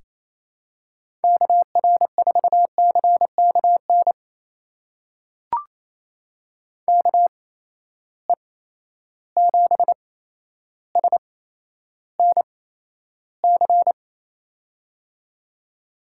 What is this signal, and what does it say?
Morse code, heard as a keyed tone, about 28 wpm, spaced as for 12 wpm, 700 Hz. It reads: KR4CKN KE7SNC
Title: USA Call Signs - 8x Character Spacing - 28wpm